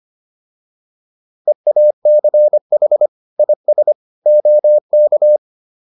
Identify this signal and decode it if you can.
Morse code, heard as a keyed tone, about 25 wpm, 600 Hz. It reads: EACH IS OK